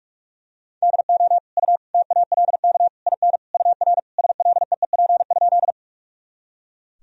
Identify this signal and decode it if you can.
Morse code, heard as a keyed tone, about 43 wpm, 700 Hz. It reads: DO U TALK IN UR SLEEP?
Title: Sentences from Top 500 Words and Abbreviations 45wpm